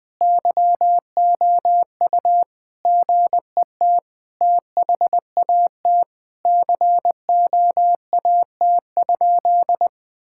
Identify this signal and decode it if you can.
Morse code, heard as a keyed tone, about 21 wpm, 700 Hz. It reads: YOU GET THAT COAT?
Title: Sentences from Top 1600 Words 20wpm